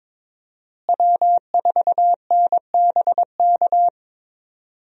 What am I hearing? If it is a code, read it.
Morse code, heard as a keyed tone, about 22 wpm, 700 Hz. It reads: W4NBK
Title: USA Call Signs - 5 Characters - Repeated 2x - 22wpm